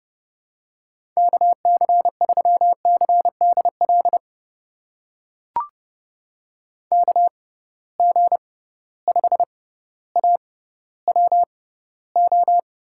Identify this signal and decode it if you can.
Morse code, heard as a keyed tone, about 30 wpm, spaced as for 15 wpm, 700 Hz. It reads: KC3CDL KG5AWO
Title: USA Call Signs - 6 Characters - 6x Character Spacing - 30wpm